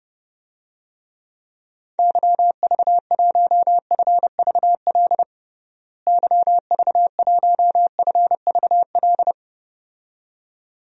Morse code, heard as a keyed tone, about 30 wpm, 700 Hz. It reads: YV1FVL YV1FVL
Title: DXCC-100 Call Signs - 6 Characters - Repeated 2x - 30wpm